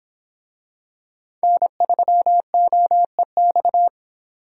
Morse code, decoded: N3OEX

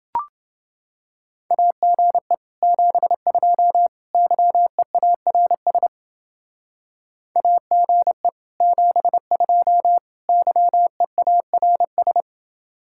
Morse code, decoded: AGE 72 YEARS AGE 72 YEARS

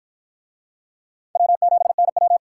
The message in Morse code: W7NW

